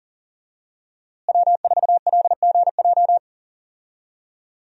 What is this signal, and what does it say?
Morse code, heard as a keyed tone, about 40 wpm, 700 Hz. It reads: W4LGJ
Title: USA Call Signs - 5 Characters - Repeated 2x - 40wpm